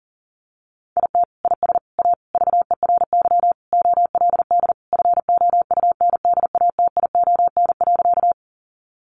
Morse code, heard as a keyed tone, about 40 wpm, 700 Hz. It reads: IT IS A VERY OLD FOUNDATION.
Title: Sentences from Top 1200 Words 40wpm